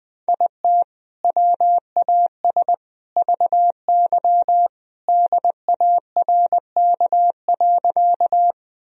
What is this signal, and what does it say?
Morse code, heard as a keyed tone, about 20 wpm, 700 Hz. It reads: IT WAS VY DARK.
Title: Sentences from Top 500 Words and Abbreviations 20wpm